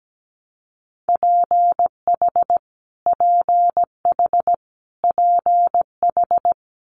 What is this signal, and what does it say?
Morse code, heard as a keyed tone, about 17 wpm, 700 Hz. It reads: PH PH PH
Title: Multi-letter Phonemes Repeated 3x 17wpm